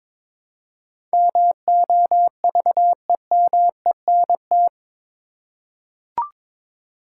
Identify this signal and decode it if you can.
Morse code, heard as a keyed tone, about 22 wpm, 700 Hz. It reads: MOVEMENT
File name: practice / 22wpm / Eight Letter Words - 22wpm